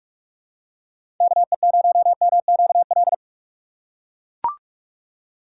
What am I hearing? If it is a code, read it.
Morse code, heard as a keyed tone, about 44 wpm, 700 Hz. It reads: KE0MQL
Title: USA Call Signs - 6 Characters - 6x Character Spacing - 45wpm